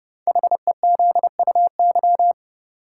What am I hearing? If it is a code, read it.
Morse code, heard as a keyed tone, about 30 wpm, 700 Hz. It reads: HEZUY
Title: Code Groups - 5 Letters - 30wpm